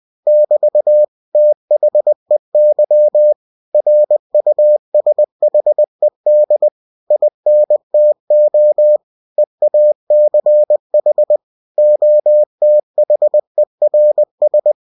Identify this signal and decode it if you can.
Morse code, heard as a keyed tone, about 20 wpm, 600 Hz. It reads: <BT> THEY RUSHED INTO EACH OTHERS
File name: cw-books / winnie-the-pooh / winnie-20wpm-page-0098